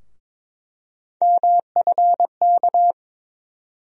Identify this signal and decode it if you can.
Morse code, heard as a keyed tone, about 22 wpm, 700 Hz. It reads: MFK
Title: USA Call Sign Postfixes - 3 Letters 22wpm